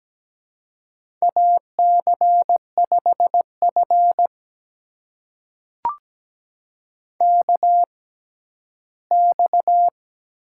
Morse code, heard as a keyed tone, about 17 wpm, spaced as for 12 wpm, 700 Hz. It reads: AC5F KX
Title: USA Call Signs - 4 Characters - 6x Character Spacing - 17wpm